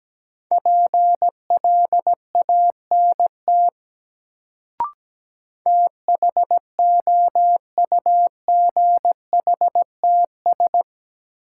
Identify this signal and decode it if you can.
Morse code, heard as a keyed tone, about 17 wpm, 700 Hz. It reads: PLANT THOUGHTS
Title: Sentences from Top 1000 Words - Review Part 1 - New Word Drill - 17wpm